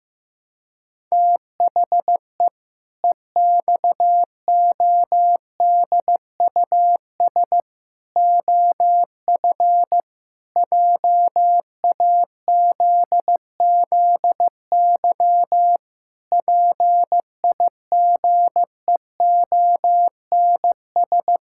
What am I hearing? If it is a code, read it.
Morse code, heard as a keyed tone, about 15 wpm, 700 Hz. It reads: THE EXODUS OF JAZZY PIGEONS
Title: Pangram Challenge v2 - 15wpm